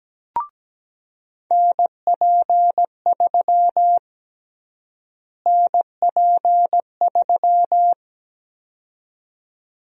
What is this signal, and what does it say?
Morse code, heard as a keyed tone, about 17 wpm, 700 Hz. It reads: NP3 NP3
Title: USA Call Sign Prefixes - Repeated 2x - 17wpm